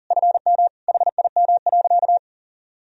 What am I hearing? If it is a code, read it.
Morse code, heard as a keyed tone, about 40 wpm, 700 Hz. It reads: FM HIM.
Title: Sentences from Top 400 Words and Abbreviations 40wpm